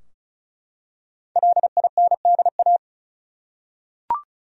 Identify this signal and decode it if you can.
Morse code, heard as a keyed tone, about 34 wpm, 700 Hz. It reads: LINDA
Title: USA Common Names - 35wpm